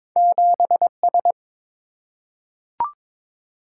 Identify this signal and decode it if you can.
Morse code, heard as a keyed tone, about 22 wpm, 700 Hz. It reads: 7S
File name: practice / 22wpm / Two Letter-Number Combinations - 22wpm